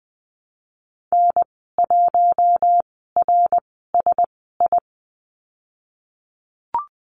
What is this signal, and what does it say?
Morse code, heard as a keyed tone, about 20 wpm, spaced as for 15 wpm, 700 Hz. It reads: N1RSI
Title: 055 Lesson 10g - Letter r - Call Signs 20wpm